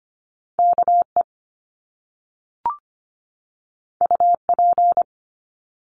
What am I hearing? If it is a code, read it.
Morse code, heard as a keyed tone, about 25 wpm, 700 Hz. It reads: KE UP
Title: Top Two Letter Combinations in Words 25wpm